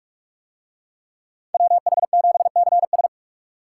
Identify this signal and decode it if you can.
Morse code, heard as a keyed tone, about 45 wpm, 700 Hz. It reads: WH7CS